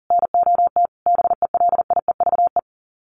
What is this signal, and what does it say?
Morse code, heard as a keyed tone, about 40 wpm, 700 Hz. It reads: NOT BELIEVE
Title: Sets of 4 Words 40wpm